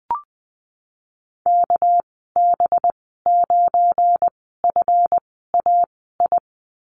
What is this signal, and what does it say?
Morse code, heard as a keyed tone, about 19 wpm, spaced as for 14 wpm, 700 Hz. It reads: KB9FAI
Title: 219 Lesson 35g - Number 0 - Call Signs 20wpm